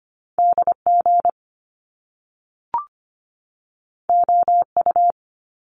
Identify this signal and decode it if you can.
Morse code, heard as a keyed tone, about 25 wpm, 700 Hz. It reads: D G O U
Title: Top Two Letter Combinations in Words 25wpm